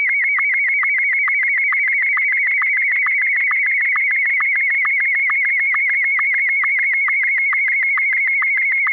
SSTV (slow-scan television), Martin 1.